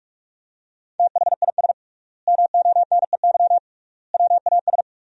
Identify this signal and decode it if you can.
Morse code, heard as a keyed tone, about 46 wpm, 700 Hz. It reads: THIS MONEY WAS
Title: 3x Word-Spacing - Sentences from Top 400 Words - 45wpm